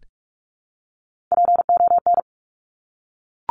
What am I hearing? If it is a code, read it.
Morse code, heard as a keyed tone, about 46 wpm, 700 Hz. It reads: PON
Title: Top Three Letter Combinations in Words 45wpm